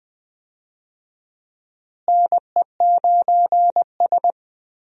Morse code, heard as a keyed tone, about 21 wpm, 700 Hz. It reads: NE9S
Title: USA Call Signs - 4 Characters - Repeated 2x - 20wpm